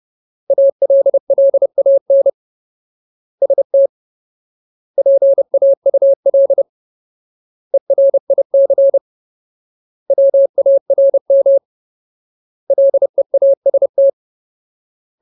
Morse code, heard as a keyed tone, about 30 wpm, 550 Hz. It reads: ALLAN ST PAUL ERIC WARM LEAST
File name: Ultimate Word List / 30WPM Ultimate